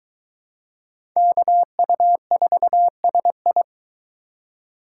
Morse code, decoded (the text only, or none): KU4SI